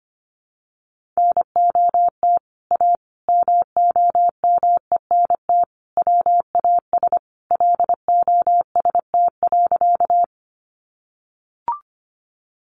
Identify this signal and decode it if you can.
Morse code, heard as a keyed tone, about 25 wpm, 700 Hz. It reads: NOT A MOMENT WAS LOST.
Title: Sentences from Top 1400 Words 25wpm